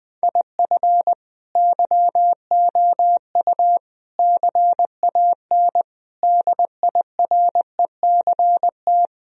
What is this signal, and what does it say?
Morse code, heard as a keyed tone, about 20 wpm, 700 Hz. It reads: IF YOU CAN DIRECT
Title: Sentences from Top 1000 Words - Review Part 2 - 20wpm